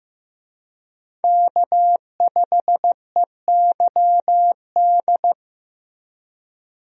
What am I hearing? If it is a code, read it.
Morse code, heard as a keyed tone, about 15 wpm, 700 Hz. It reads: K5EYD
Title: USA Call Signs - 5 Characters - Repeated 2x - 15wpm